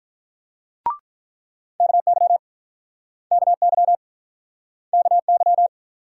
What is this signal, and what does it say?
Morse code, heard as a keyed tone, about 46 wpm, 700 Hz. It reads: KY KY KY